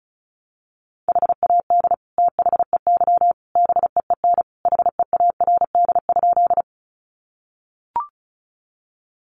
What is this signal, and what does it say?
Morse code, heard as a keyed tone, about 34 wpm, 700 Hz. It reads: HAD THEY BEEN HEARD?
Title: Sentences from Top 400 Words 35wpm